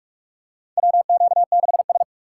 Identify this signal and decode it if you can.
Morse code, heard as a keyed tone, about 45 wpm, 700 Hz. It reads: WQ6S
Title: USA Call Signs - 3x Character Spacing - 45wpm